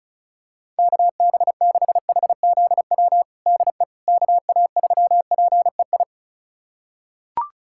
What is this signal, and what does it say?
Morse code, heard as a keyed tone, about 36 wpm, 700 Hz. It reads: KB6HZW DE KA3PEI
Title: USA Callsign DE Callsign - 6 Characters - 35wpm